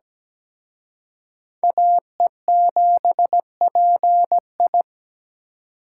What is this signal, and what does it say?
Morse code, heard as a keyed tone, about 20 wpm, 700 Hz. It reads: AE7PI
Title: Speed Racing - USA 2x2 Call Signs 17wpm